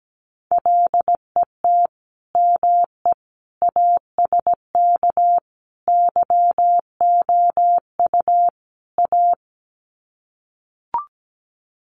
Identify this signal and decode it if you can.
Morse code, heard as a keyed tone, about 17 wpm, 700 Hz. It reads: LET ME ASK YOU A
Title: Sets of 5 Words Encore 17wpm